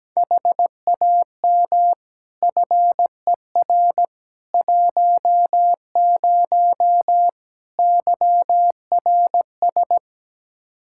Morse code, heard as a keyed tone, about 17 wpm, 700 Hz. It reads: HAM FER 10 YRS
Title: K1USN SST QSO Practice 17wpm